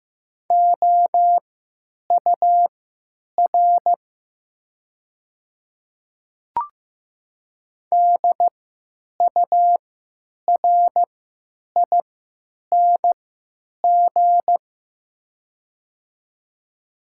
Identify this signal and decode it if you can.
Morse code, heard as a keyed tone, about 15 wpm, spaced as for 9 wpm, 700 Hz. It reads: OUR DURING
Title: 3x Character Spacing - Top 500 Words - 15wpm